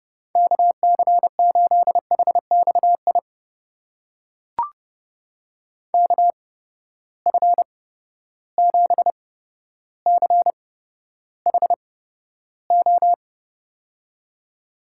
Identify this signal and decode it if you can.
Morse code, heard as a keyed tone, about 30 wpm, spaced as for 13 wpm, 700 Hz. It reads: KC8HXI KF7CHO
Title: USA Call Signs - 6 Characters - 8x Character Spacing - 30wpm